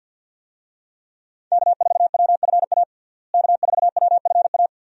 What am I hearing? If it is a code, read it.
Morse code, heard as a keyed tone, about 53 wpm, 700 Hz. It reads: K4WFA K4WFA
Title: USA Call Signs - 5 Characters - Repeated 2x - 50wpm